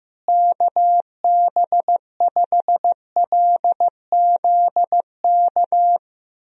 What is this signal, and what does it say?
Morse code, heard as a keyed tone, about 16 wpm, 700 Hz. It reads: KB5LZK
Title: USA Call Signs - 6 Characters - Repeated 2x - 15wpm